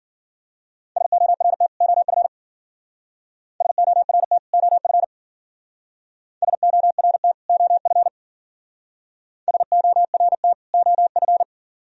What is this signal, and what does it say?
Morse code, heard as a keyed tone, about 48 wpm, 700 Hz. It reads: SORT OF SORT OF SORT OF SORT OF
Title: Speed-Racing - Sets of 2 Words 40wpm